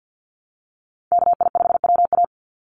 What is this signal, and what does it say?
Morse code, heard as a keyed tone, about 49 wpm, 700 Hz. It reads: KI5WA
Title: NA SOTA Chaser Callsigns 50wpm